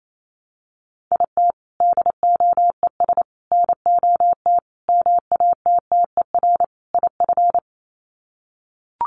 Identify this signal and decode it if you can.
Morse code, heard as a keyed tone, about 28 wpm, 690 Hz. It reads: IT DOES NOT MATTER IF
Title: Sets of 5 Words Encore 28wpm